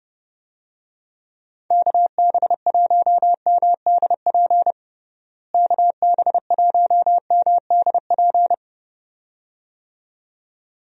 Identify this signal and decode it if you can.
Morse code, heard as a keyed tone, about 30 wpm, 700 Hz. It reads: KB1MDP KB1MDP
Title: USA Call Signs - 6 Characters - Repeated 2x - 30wpm